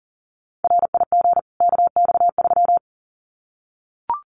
Morse code, heard as a keyed tone, about 40 wpm, 700 Hz. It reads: RIG KX3
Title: Common QSO Phrases 40wpm